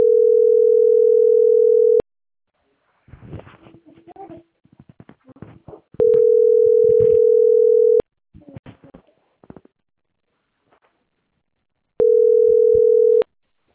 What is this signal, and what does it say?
Ringback tone (North America), 440 + 480 Hz.